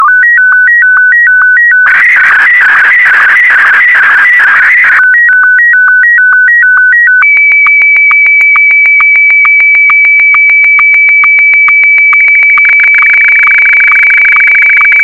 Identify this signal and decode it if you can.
SSTV (slow-scan television), Martin 1.